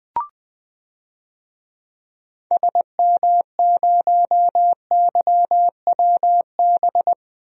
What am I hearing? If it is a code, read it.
Morse code, heard as a keyed tone, about 21 wpm, 700 Hz. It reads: SM0YWB
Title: DXCC-100 Call Signs - 6 Characters - 20wpm